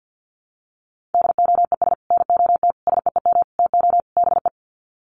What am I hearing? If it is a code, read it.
Morse code, heard as a keyed tone, about 50 wpm, 700 Hz. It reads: DOES NOT SEEM TO BE
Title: Sets of 5 Words 50wpm